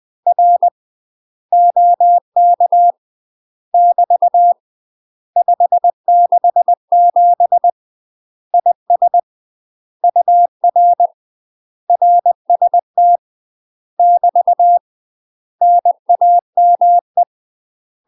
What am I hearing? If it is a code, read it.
Morse code, heard as a keyed tone, about 20 wpm, 700 Hz. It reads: R OK <BT> 567 IS UR RST <BT> NAME
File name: UK MASTER QSOs TRAINING / 20WPM / UK MASTER QSOs 0073